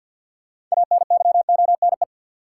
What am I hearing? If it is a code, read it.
Morse code, heard as a keyed tone, about 48 wpm, 700 Hz. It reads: ANYONE
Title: Sets of 5 Words Encore - New Word Drill - 50wpm